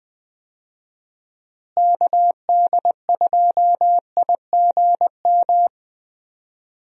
Morse code, heard as a keyed tone, about 20 wpm, 700 Hz. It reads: KD2IGM
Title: USA Call Signs - 6 Characters - Repeated 2x - 20wpm